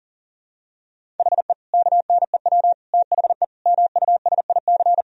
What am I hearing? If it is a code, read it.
Morse code, heard as a keyed tone, about 37 wpm, 700 Hz. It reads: HE KNEW THE MUSIC
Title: Sentences from Top 1000 Words - Review Part 1 - 40wpm